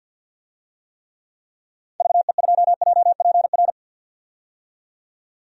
Morse code, heard as a keyed tone, about 47 wpm, 700 Hz. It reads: VE2JPR
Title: DXCC-100 Call Signs - 6 Characters - 50wpm